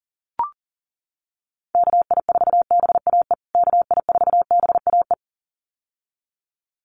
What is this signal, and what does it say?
Morse code, heard as a keyed tone, about 40 wpm, 700 Hz. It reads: KI4BAE KI4BAE
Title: USA Call Signs Repeated 2x 40wpm